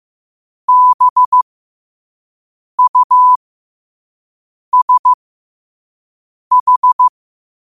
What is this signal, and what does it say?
Morse code, heard as a keyed tone, about 15 wpm, spaced as for 5 wpm, 1000 Hz. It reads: BUSH
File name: qotdfiles / 5wpm